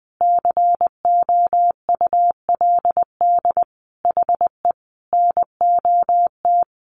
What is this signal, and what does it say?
Morse code, heard as a keyed tone, about 20 wpm, 700 Hz. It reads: COULD HE NOT
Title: Sentences from Top 600 Words 20wpm